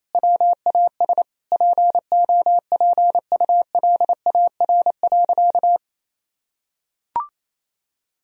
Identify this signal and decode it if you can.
Morse code, heard as a keyed tone, about 28 wpm, 700 Hz. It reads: WAS POPULAR.
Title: Sentences from Top 1000 Words - Review Part 3 - 28wpm